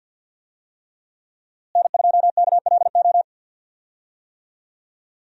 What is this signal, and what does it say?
Morse code, heard as a keyed tone, about 48 wpm, 700 Hz. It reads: N2KLO
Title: USA Call Signs - 5 Characters - Repeated 3x - 50wpm